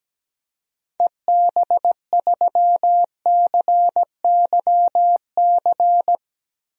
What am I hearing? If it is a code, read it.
Morse code, heard as a keyed tone, about 17 wpm, 700 Hz. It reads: EB3CYC
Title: DXCC-100 Call Signs - 6 Characters - Repeated 3x - 17wpm